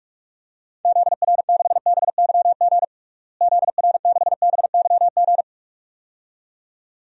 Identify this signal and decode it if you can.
Morse code, heard as a keyed tone, about 46 wpm, 700 Hz. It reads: ZR6BYG ZR6BYG
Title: DXCC-100 Call Signs - 6 Characters - Repeated 2x - 45wpm